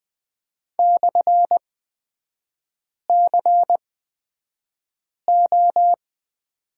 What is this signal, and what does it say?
Morse code, heard as a keyed tone, about 20 wpm, spaced as for 5 wpm, 700 Hz. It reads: /CO
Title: 236 Lesson 38e - Letter q - Three Characters 20wpm